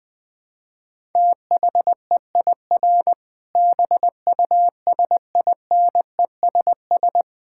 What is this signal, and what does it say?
Morse code, heard as a keyed tone, about 21 wpm, 700 Hz. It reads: THEIR BUSINESS
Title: Sentences from Top 1000 Words - Review Part 5 - 20wpm